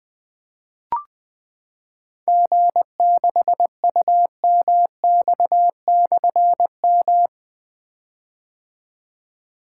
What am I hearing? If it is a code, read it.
Morse code, heard as a keyed tone, about 20 wpm, 700 Hz. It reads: G6UMX/M
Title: Difficult Call Signs 20wpm